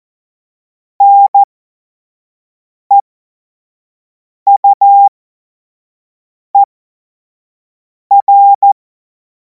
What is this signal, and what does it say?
Morse code, heard as a keyed tone, about 14 wpm, spaced as for 5 wpm, 800 Hz. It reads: NEUER
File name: Morsekurs / Witze / Witz 30. Juni 2026 25 F70